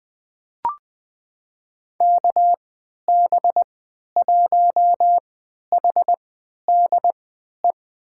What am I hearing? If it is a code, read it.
Morse code, heard as a keyed tone, about 20 wpm, spaced as for 11 wpm, 700 Hz. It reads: KB1HDE
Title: USA Call Signs - 6 Characters - 3x Character Spacing - 20wpm